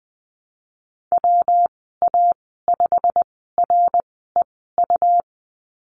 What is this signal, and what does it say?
Morse code, heard as a keyed tone, about 19 wpm, spaced as for 14 wpm, 700 Hz. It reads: WA5REU
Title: 122 Lesson 20g - Number 3 - Call Signs 20wpm